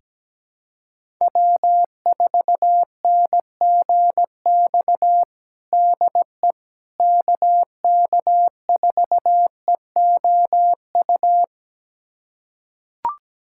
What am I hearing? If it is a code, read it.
Morse code, heard as a keyed tone, about 17 wpm, 700 Hz. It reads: W4NGX DE KK4EOU
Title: USA Callsign DE Callsign 17wpm